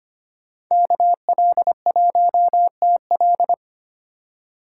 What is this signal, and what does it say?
Morse code, heard as a keyed tone, about 31 wpm, 700 Hz. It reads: KL1TL